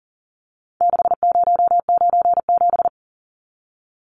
Morse code, heard as a keyed tone, about 39 wpm, 700 Hz. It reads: 6097
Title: Numbers - 4-Digit 40wpm